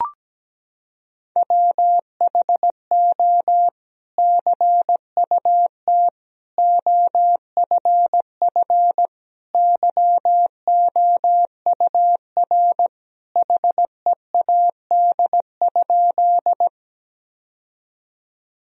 Morse code, heard as a keyed tone, about 17 wpm, 700 Hz. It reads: WHO CUT OFF YOUR HEAD?
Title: Sentences from Top 1000 Words - Review Part 1 - 17wpm